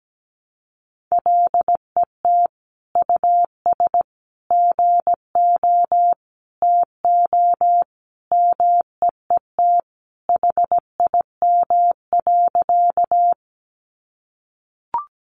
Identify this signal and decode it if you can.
Morse code, heard as a keyed tone, about 17 wpm, 700 Hz. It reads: LET US GO TO MEET HIM.
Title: Sentences from Top 1100 Words 17wpm